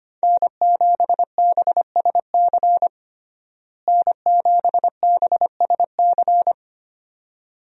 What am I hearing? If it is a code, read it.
Morse code, heard as a keyed tone, about 25 wpm, 700 Hz. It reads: N7BSC N7BSC
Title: USA Call Signs - 5 Characters - Repeated 2x - 25wpm